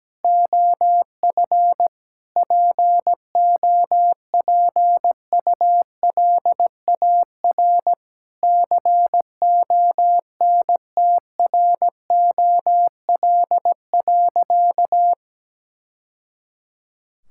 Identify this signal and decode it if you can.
Morse code, heard as a keyed tone, about 17 wpm, 700 Hz. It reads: OF POPULAR CONTROL.
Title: Sentences from Top 1000 Words - Review Part 5 - 17wpm